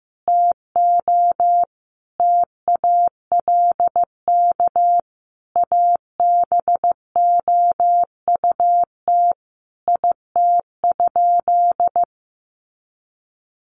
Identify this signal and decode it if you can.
Morse code, heard as a keyed tone, about 15 wpm, 700 Hz. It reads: TO TALK ABOUT IT?
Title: Sentences from Top 1400 Words 15wpm